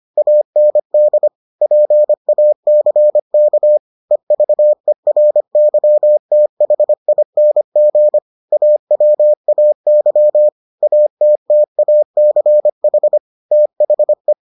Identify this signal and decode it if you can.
Morse code, heard as a keyed tone, about 25 wpm, 600 Hz. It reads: AND PACK EVERYTHING AWAY ATTACH THE